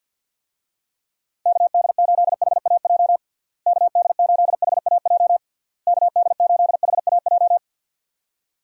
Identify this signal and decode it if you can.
Morse code, heard as a keyed tone, about 51 wpm, 700 Hz. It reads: KD8HAJ KD8HAJ KD8HAJ
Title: USA Call Signs - 6 Characters - Repeated 3x - 50wpm